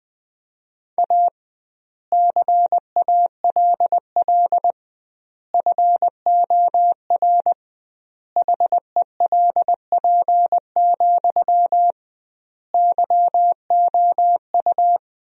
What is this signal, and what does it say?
Morse code, heard as a keyed tone, about 20 wpm, 700 Hz. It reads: A CALL FOR HELP, YOU
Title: 2x Word-Spacing - Sentences from Top 300 Words - 20wpm